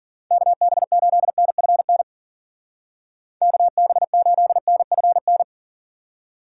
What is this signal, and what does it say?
Morse code, heard as a keyed tone, about 50 wpm, 700 Hz. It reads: KB8NFN KB8NFN
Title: Speed Racing - USA 2x3 Call Signs 40wpm